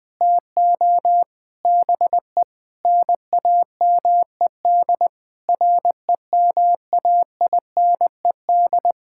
Morse code, heard as a keyed tone, about 20 wpm, 700 Hz. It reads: TO BE NAMED REMAINED
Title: Sentences with 7-Word Phrases - 20wpm